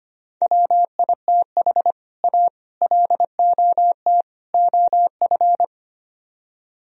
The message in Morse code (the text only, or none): WITH A LOT OF